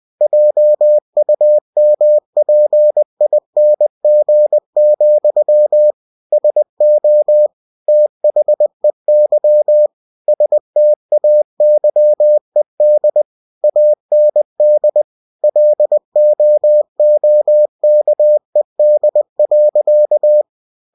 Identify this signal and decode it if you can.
Morse code, heard as a keyed tone, about 20 wpm, 600 Hz. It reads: JUMPING, SO THEY STAYED AND LOOKED.
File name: cw-books / winnie-the-pooh / winnie-20wpm-page-0066